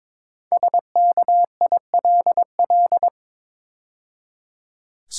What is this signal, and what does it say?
Morse code, heard as a keyed tone, about 22 wpm, 690 Hz. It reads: SKILL
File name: 5 Letter Words - First 3 Letters Given - 22wpm